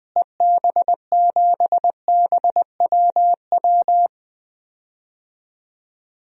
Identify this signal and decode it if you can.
Morse code, heard as a keyed tone, about 20 wpm, 700 Hz. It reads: EB7BWW
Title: DXCC-100 Call Signs - 6 Characters - Repeated 3x - 20wpm